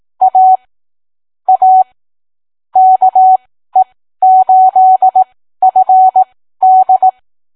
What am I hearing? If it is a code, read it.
Morse code, heard as a keyed tone, about 18 wpm, spaced as for 13 wpm, 750 Hz. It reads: A A KE8FD